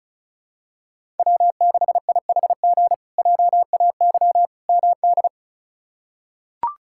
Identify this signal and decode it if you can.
Morse code, heard as a keyed tone, about 36 wpm, 700 Hz. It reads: W6IHG JAY MD
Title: K1USN SST QSO Practice - 35wpm